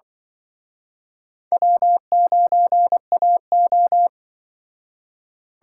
Morse code, heard as a keyed tone, about 24 wpm, 700 Hz. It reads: W9AO
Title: Speed Racing - USA 1x2 Call Signs 20wpm